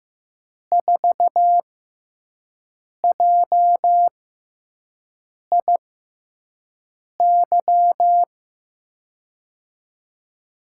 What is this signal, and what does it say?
Morse code, heard as a keyed tone, about 15 wpm, spaced as for 5 wpm, 700 Hz. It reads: 4JIY
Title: USA Call Signs - 6 Characters - 6x Character Spacing - 15wpm